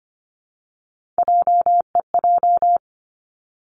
Morse code, heard as a keyed tone, about 25 wpm, 700 Hz. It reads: JEJ
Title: Three Challenging Letter-Number Combinations 25wpm